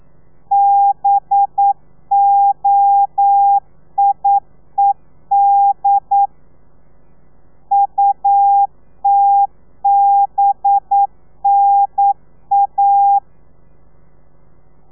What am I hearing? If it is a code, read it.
Morse code, heard as a keyed tone, about 9 wpm, 800 Hz. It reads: BOIED UTBNA
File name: telegrafia / i2tfj / Lez4 40